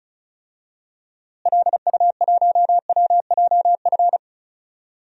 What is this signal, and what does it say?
Morse code, heard as a keyed tone, about 36 wpm, 700 Hz. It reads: LU1WJF
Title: DXCC-100 Call Signs - 6 Characters - Repeated 2x - 35wpm